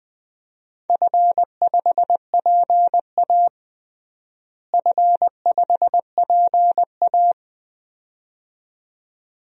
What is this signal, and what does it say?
Morse code, heard as a keyed tone, about 20 wpm, 700 Hz. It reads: F5PA F5PA
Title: DXCC-100 Call Signs - 4 Characters - Repeated 2x - 20wpm